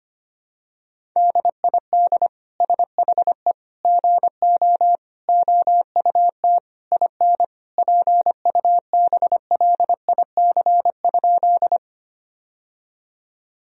Morse code, heard as a keyed tone, about 25 wpm, 700 Hz. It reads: DID SHE GO OUT IN PUBLIC?
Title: Sentences from Top 1000 Words - Review Part 3 - 25wpm